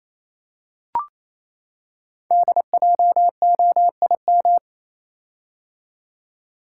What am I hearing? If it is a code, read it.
Morse code, heard as a keyed tone, about 27 wpm, 700 Hz. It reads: DJOIM
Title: Code Groups - 5 Letters - 28wpm